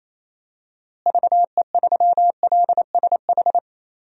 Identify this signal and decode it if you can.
Morse code, heard as a keyed tone, about 28 wpm, 700 Hz. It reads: VE3LSH
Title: DXCC-100 Call Signs - 6 Characters - Repeated 3x - 28wpm